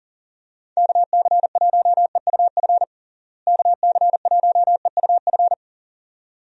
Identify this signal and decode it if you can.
Morse code, heard as a keyed tone, about 40 wpm, 700 Hz. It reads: KC1EUF KC1EUF
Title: USA Call Signs - 6 Characters - Repeated 2x - 40wpm